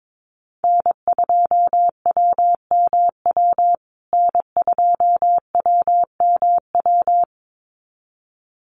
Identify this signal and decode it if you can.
Morse code, heard as a keyed tone, about 22 wpm, 700 Hz. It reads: N2WMW N2WMW
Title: USA Call Signs Repeated 2x 22wpm